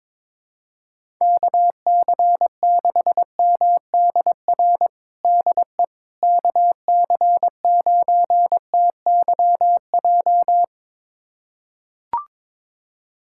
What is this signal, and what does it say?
Morse code, heard as a keyed tone, about 22 wpm, 700 Hz. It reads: KC6MDR DE KC9TYJ
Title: USA Callsign DE Callsign - 6 Characters - 22wpm